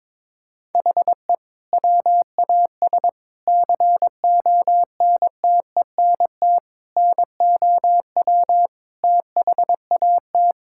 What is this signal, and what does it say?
Morse code, heard as a keyed tone, about 22 wpm, 700 Hz. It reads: HE WAS CONTENT NOW THAT
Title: Sentences from Top 1000 Words - Review Part 6 - 22wpm